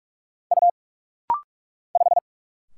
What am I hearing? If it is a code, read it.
Morse code, heard as a keyed tone, about 44 wpm, 700 Hz. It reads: U 5
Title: Single Letter-Number - Rapid-Fire 45wpm